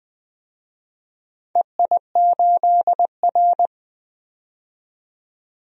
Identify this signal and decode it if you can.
Morse code, heard as a keyed tone, about 21 wpm, 700 Hz. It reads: EI8R
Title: DXCC-100 Call Signs - 4 Characters - 20wpm